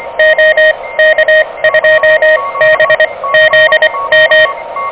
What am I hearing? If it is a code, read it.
Morse code, heard as a keyed tone, about 24 wpm, spaced as for 19 wpm, 590 Hz, with noise. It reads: OK2BZM
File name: OK QRP 2023 / OK QRP 2023